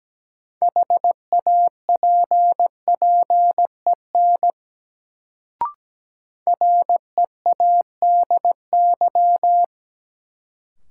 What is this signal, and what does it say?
Morse code, heard as a keyed tone, about 17 wpm, 700 Hz. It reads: HAPPEN READY
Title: Sentences from Top 400 Words - New Word Drill - 17wpm